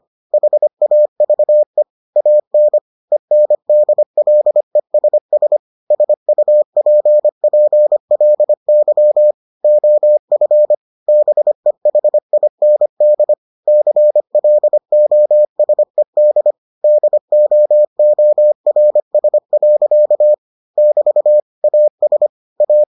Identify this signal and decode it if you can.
Morse code, heard as a keyed tone, about 25 wpm, 600 Hz. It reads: HAVE AN ENDLESS SUPPLY OF BEHIND CLOSED DOORS. <BT> AS A